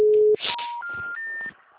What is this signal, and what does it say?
Special information tone.
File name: out- -2010- - - .38494